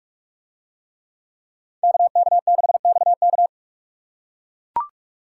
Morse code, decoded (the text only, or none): KK6XK